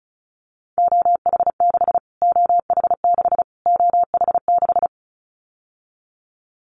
Morse code, heard as a keyed tone, about 34 wpm, 700 Hz. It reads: OH6 OH6 OH6
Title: Callsign Prefixes for Top 25 DXCC - Repeated 3x - 35wpm